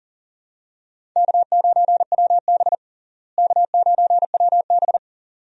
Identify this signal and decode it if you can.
Morse code, heard as a keyed tone, about 40 wpm, 700 Hz. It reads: K9WB K9WB
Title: USA Call Signs - 4 Characters - Repeated 2x - 40wpm